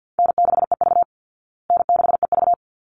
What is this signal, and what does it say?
Morse code, heard as a keyed tone, about 50 wpm, 700 Hz. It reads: N6EV N6EV
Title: Easy Call Signs Repeated 2x - 50wpm